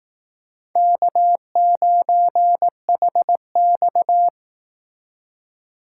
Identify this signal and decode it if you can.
Morse code, heard as a keyed tone, about 18 wpm, 700 Hz. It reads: K9HX
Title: Speed Racing - USA 1x2 Call Signs 15wpm